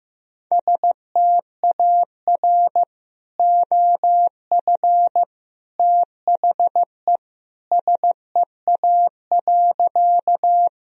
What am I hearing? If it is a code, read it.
Morse code, heard as a keyed tone, about 16 wpm, 700 Hz. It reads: STAR OF THE SEA.
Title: Sentences from Top 1000 Words - Review Part 2 - 15wpm